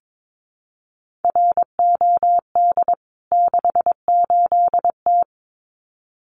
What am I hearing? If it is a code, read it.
Morse code, heard as a keyed tone, about 22 wpm, 700 Hz. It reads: ROD 68T
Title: CWT Exchange 22wpm